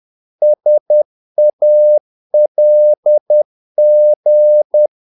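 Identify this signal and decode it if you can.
Morse code, heard as a keyed tone, about 10 wpm, 600 Hz. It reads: SALG
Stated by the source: Morse practice of short Danish words at 10 wpm, from CW med Gnister.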